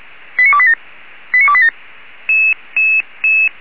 Five-tone selective call (ZVEI), heard as quick runs of five tones, tones of about 70 ms. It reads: 89287 89287